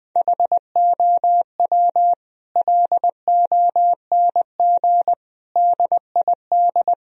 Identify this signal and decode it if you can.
Morse code, heard as a keyed tone, about 20 wpm, 700 Hz. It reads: HOW LONG DID
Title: Sentences from Top 1000 Words - Review Part 2 - 20wpm